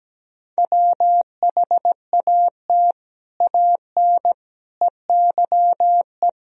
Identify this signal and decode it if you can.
Morse code, heard as a keyed tone, about 17 wpm, 700 Hz. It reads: WHAT AN EYE
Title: Sentences from Top 1000 Words - Review Part 1 - 17wpm